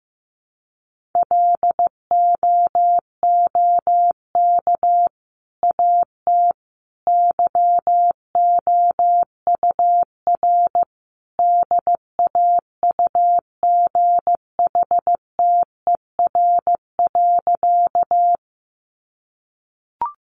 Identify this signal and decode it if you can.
Morse code, heard as a keyed tone, about 15 wpm, 700 Hz. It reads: LOOK AT YOUR DAUGHTER.
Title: Sentences from Top 1500 Words 15wpm